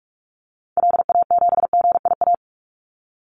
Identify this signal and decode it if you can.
Morse code, heard as a keyed tone, about 44 wpm, 700 Hz. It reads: LA7GIA
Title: Call Signs Repeated 2x - 45wpm